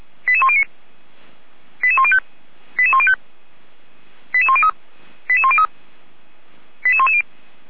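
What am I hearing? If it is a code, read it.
Five-tone selective call (ZVEI), heard as quick runs of five tones, tones of about 70 ms. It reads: 88109 88105 88105 88103 88103 88110
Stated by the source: Five-tone ZVEI selective call 88104, received by SDR, feuerwehr-orlamuende fire brigade.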